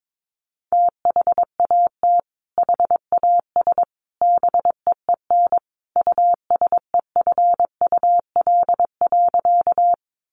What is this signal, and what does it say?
Morse code, heard as a keyed tone, about 22 wpm, 700 Hz. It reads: THAT HAS BEEN USEFUL.
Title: Sentences from Top 700 Words 22wpm